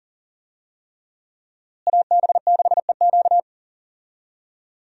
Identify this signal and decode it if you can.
Morse code, heard as a keyed tone, about 37 wpm, 700 Hz. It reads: AB6EQ
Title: USA Call Signs - 5 Characters - Repeated 3x - 40wpm